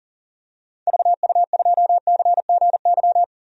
Morse code, heard as a keyed tone, about 40 wpm, 700 Hz. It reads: VU2CGY
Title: DXCC-100 Call Signs - 6 Characters - 40wpm